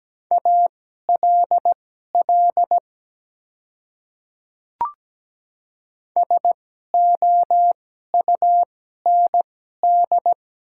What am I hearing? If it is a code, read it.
Morse code, heard as a keyed tone, about 17 wpm, spaced as for 12 wpm, 690 Hz. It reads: ALL SOUND
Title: 2x Character Spacing - Top 100 Words - 17wpm